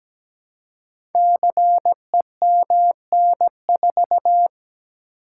Morse code, heard as a keyed tone, about 17 wpm, 700 Hz. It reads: CEMN4
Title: Code Groups - 5 Letter:Numbers - 17wpm